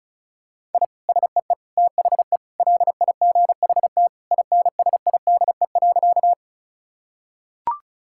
Morse code, heard as a keyed tone, about 34 wpm, 700 Hz. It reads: I SEE THE LIGHT INSIDE.
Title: Sentences from Top 1000 Words - Review Part 6 - 35wpm